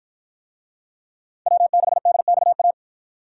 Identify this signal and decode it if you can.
Morse code, heard as a keyed tone, about 53 wpm, 700 Hz. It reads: W6DXA